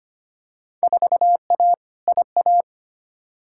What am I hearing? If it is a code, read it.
Morse code, heard as a keyed tone, about 25 wpm, 700 Hz. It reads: 4A IA
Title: Field Day Exchanges - 25wpm